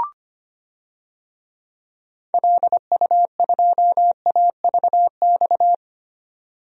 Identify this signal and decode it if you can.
Morse code, heard as a keyed tone, about 25 wpm, 700 Hz. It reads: LU2AVX
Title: DXCC-100 Call Signs - 6 Characters - 25wpm